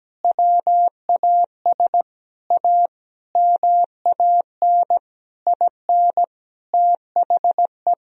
Morse code, heard as a keyed tone, about 17 wpm, 700 Hz. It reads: WAS A MAN IN THE
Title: Sentences from Top 1000 Words - Review Part 3 - 17wpm